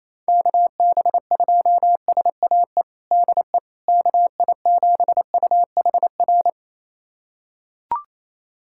Morse code, heard as a keyed tone, about 28 wpm, 700 Hz. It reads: KB2SAE DE KI7UHR
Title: USA Callsign DE Callsign - 6 Characters - 28wpm